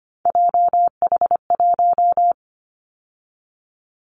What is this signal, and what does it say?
Morse code, heard as a keyed tone, about 25 wpm, 700 Hz. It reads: JH1
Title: Callsign Prefixes for Top 25 DXCC - 25wpm